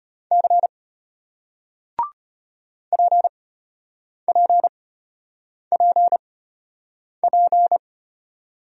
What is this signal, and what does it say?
Morse code, heard as a keyed tone, about 33 wpm, spaced as for 8 wpm, 700 Hz. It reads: C PPPP